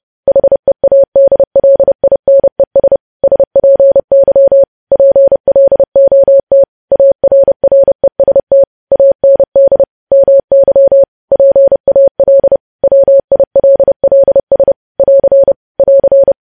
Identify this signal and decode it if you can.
Morse code, heard as a keyed tone, about 30 wpm, 550 Hz. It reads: HEADLINES SPY PLOT ARREST AND MY PAL WILLS <AR> <AR>
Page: news headlines in morse code at 30 wpm